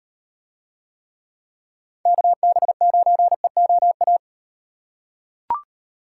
Morse code, heard as a keyed tone, about 38 wpm, 700 Hz. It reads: KB9EOA